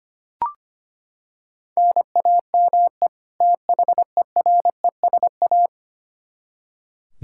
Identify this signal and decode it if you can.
Morse code, heard as a keyed tone, about 25 wpm, 700 Hz. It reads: NAME THERESA
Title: K1USN SST QSO Practice 25wpm